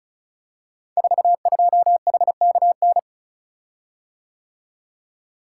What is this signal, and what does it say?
Morse code, heard as a keyed tone, about 36 wpm, 700 Hz. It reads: 42HKN
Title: Code Groups - 5 Letter:Numbers - 35wpm